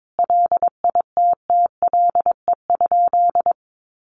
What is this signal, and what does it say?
Morse code, heard as a keyed tone, about 22 wpm, 700 Hz. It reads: LITTLE?
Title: Sentences from Top 1400 Words 22wpm